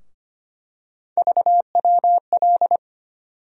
Morse code, heard as a keyed tone, about 25 wpm, 700 Hz. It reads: VWL
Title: USA Call Sign Postfixes - 3 Letters 25wpm